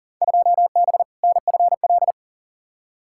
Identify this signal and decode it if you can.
Morse code, heard as a keyed tone, about 40 wpm, 700 Hz. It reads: 2B NFL
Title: Field Day Exchanges Encore - 40wpm